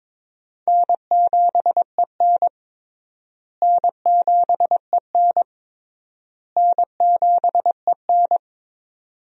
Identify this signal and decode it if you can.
Morse code, heard as a keyed tone, about 22 wpm, 700 Hz. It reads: N7EN N7EN N7EN
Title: USA Call Signs - 4 Characters - Repeated 3x - 22wpm